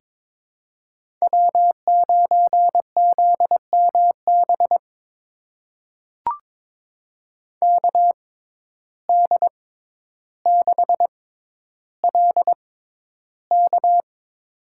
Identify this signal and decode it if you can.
Morse code, heard as a keyed tone, about 21 wpm, spaced as for 11 wpm, 700 Hz. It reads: W9ZMB KD6LK
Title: USA Call Signs - 5 Characters - 6x Character Spacing - 22wpm